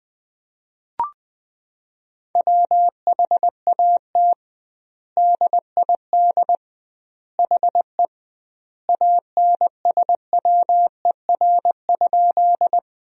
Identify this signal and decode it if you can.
Morse code, heard as a keyed tone, about 21 wpm, 700 Hz. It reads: WHAT DID HE ANSWER?
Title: 2x Word-Spacing - Sentences from Top 300 Words - 20wpm